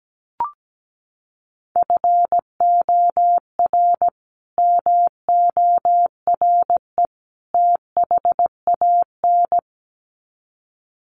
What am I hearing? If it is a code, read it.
Morse code, heard as a keyed tone, about 17 wpm, 700 Hz. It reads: FOR MORE THAN
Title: Sets of 3 Words 17wpm